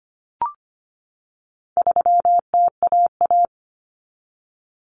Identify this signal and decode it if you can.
Morse code, heard as a keyed tone, about 25 wpm, 700 Hz. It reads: 3TAA
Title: Common Cut Numbers - 4-Digit 25wpm